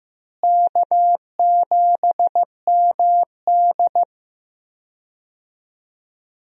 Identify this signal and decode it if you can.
Morse code, heard as a keyed tone, about 15 wpm, 700 Hz. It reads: K7MD
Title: USA Call Signs - 4 Characters - Repeated 2x - 15wpm